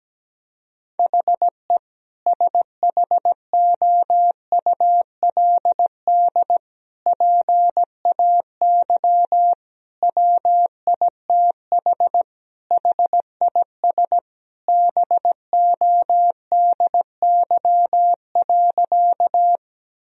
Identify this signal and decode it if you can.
Morse code, heard as a keyed tone, about 17 wpm, 700 Hz. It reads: HE SHOULD PAY WITH HIS BODY.
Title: Sentences from Top 1000 Words - Review Part 2 - 17wpm